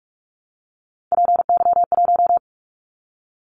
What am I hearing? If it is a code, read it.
Morse code, heard as a keyed tone, about 46 wpm, 700 Hz. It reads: PY1
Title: Callsign Prefixes for Top 50 DXCC - 45wpm